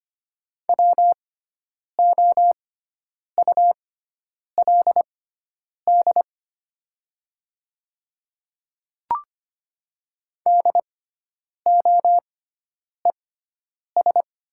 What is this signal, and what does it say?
Morse code, heard as a keyed tone, about 25 wpm, spaced as for 9 wpm, 700 Hz. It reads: WOULD DOES
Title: 6x Character Spacing - Top 300 Words - 25wpm